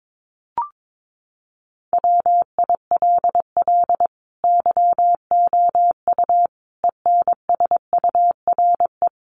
Morse code, heard as a keyed tone, about 22 wpm, 700 Hz. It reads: WILL YOU ENSURE
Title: Sentences from Top 1100 Words 22wpm